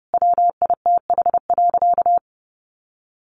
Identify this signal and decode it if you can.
Morse code, heard as a keyed tone, about 30 wpm, 700 Hz. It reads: WITH.
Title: Sentences from Top 1500 Words 30wpm